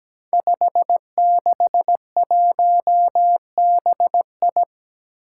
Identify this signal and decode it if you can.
Morse code, heard as a keyed tone, about 17 wpm, 700 Hz. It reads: 561BI
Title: Five Challenging Letter-Number Combinations 17wpm